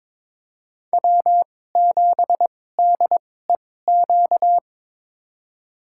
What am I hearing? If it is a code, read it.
Morse code, heard as a keyed tone, about 22 wpm, spaced as for 16 wpm, 700 Hz. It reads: W7DEQ
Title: USA Call Signs - 5 Characters - 2x Character Spacing - 22wpm